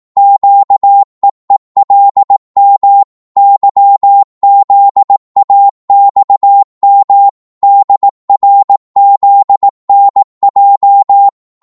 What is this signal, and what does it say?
Morse code, heard as a keyed tone, about 18 wpm, 800 Hz. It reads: QEELM YZAXM DRZNJ
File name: Morse / ltr-18wpm